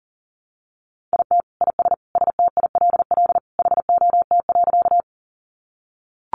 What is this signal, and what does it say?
Morse code, heard as a keyed tone, about 40 wpm, 700 Hz. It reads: IT IS STILL HOT.
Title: Additional Easy Phrases 40wpm